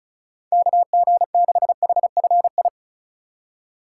Morse code, heard as a keyed tone, about 36 wpm, 700 Hz. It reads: KG6HFI
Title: USA Call Signs - 6 Characters - 3x Character Spacing - 35wpm